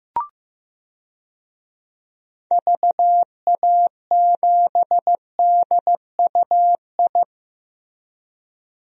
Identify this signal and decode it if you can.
Morse code, heard as a keyed tone, about 16 wpm, 700 Hz. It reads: VA7DUI
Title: DXCC-100 Call Signs - 6 Characters - 15wpm